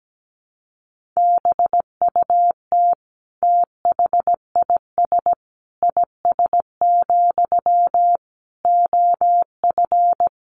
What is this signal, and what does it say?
Morse code, heard as a keyed tone, about 17 wpm, 700 Hz. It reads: BUT THIS IS, OF
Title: Sentences from Top 1500 Words 17wpm